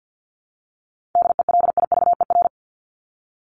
Morse code, heard as a keyed tone, about 48 wpm, 700 Hz. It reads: DELIVER
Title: Top 1751-2000 Words 50wpm